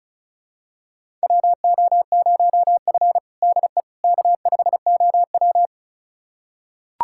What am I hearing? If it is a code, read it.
Morse code, heard as a keyed tone, about 36 wpm, 700 Hz. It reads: WO0F DE K5OW
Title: USA Callsign DE Callsign - 4 Characters - 35wpm